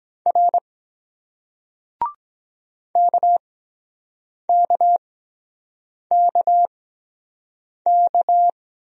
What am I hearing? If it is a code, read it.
Morse code, heard as a keyed tone, about 23 wpm, spaced as for 7 wpm, 700 Hz. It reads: R KKKK